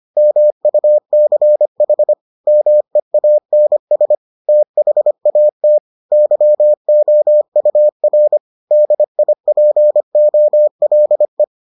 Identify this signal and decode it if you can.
Morse code, heard as a keyed tone, about 25 wpm, 600 Hz. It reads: MUCH MEANS THAT YOUR DIPOLE